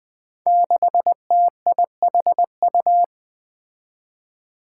Morse code, heard as a keyed tone, about 21 wpm, 700 Hz. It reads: 6TIHU
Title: Five Challenging Letter-Number Combinations 20wpm